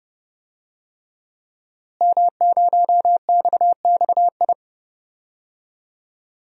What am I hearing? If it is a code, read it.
Morse code, heard as a keyed tone, about 32 wpm, 700 Hz. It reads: M0XXI
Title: DXCC-100 Call Signs - 5 Characters - Repeated 3x - 30wpm